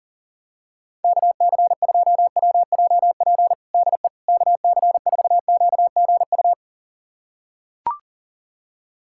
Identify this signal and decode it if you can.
Morse code, heard as a keyed tone, about 40 wpm, 700 Hz. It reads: KC2WJP DE KC4QGU
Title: USA Callsign DE Callsign - 6 Characters - 40wpm